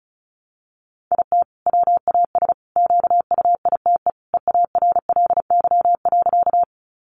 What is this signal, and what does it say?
Morse code, heard as a keyed tone, about 34 wpm, 700 Hz. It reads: IT WAS QUITE EARLY.
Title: Sentences from Top 700 Words 35wpm